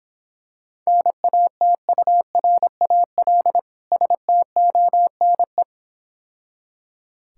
Morse code, heard as a keyed tone, about 26 wpm, 700 Hz. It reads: NATURAL STONE